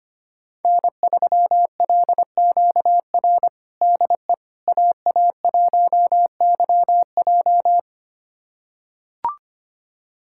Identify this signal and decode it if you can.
Morse code, heard as a keyed tone, about 25 wpm, 700 Hz. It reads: N3LQR DE AA1YJ
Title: USA Callsign DE Callsign - 5 Characters - 25wpm